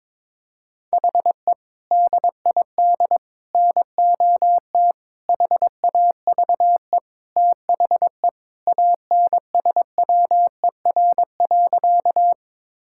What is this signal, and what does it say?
Morse code, heard as a keyed tone, about 21 wpm, 700 Hz. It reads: HE DID NOT HAVE THE ANSWER.
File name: practice / 22wpm / Sentences with 4-Word Phrases - 22wpm